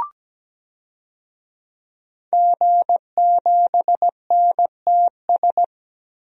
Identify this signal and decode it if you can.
Morse code, heard as a keyed tone, about 17 wpm, 700 Hz. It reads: G7NTS
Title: DXCC-100 Call Signs - 5 Characters - 17wpm